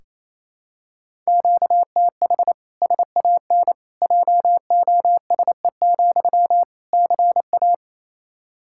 Morse code, heard as a keyed tone, about 28 wpm, 700 Hz. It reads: QTH SAN JOSE, CA
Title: Practice QSO - Exchange 1 - Beginning - 28wpm